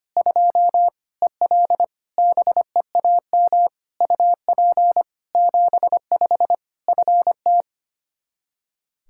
Morse code, heard as a keyed tone, about 25 wpm, 700 Hz. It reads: 2 EL BEAM UP 75 FT
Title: Practice QSO - Exchange 2 - Info - 25wpm